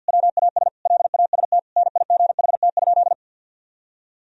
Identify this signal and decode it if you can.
Morse code, heard as a keyed tone, about 50 wpm, 700 Hz. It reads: WAS LAST NIGHT?
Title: Sentences from Top 1000 Words - Review Part 1 - 50wpm